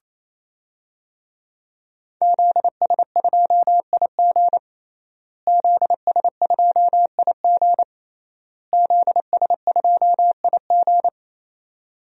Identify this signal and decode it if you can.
Morse code, heard as a keyed tone, about 28 wpm, 700 Hz. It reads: ZS2IG ZS2IG ZS2IG
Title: DXCC-100 Call Signs - 5 Characters - Repeated 3x - 28wpm